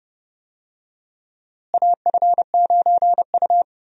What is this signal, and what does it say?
Morse code, heard as a keyed tone, about 31 wpm, 700 Hz. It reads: AF9U
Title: USA Call Signs - 4 Characters - Repeated 2x - 30wpm